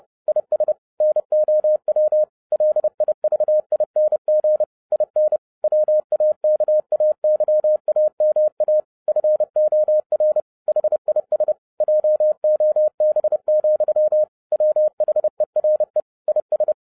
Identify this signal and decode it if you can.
Morse code, heard as a keyed tone, about 30 wpm, 610 Hz. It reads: IS NOW LIVING IN WAKAYAMA FOR HIS JOB, WHERE IS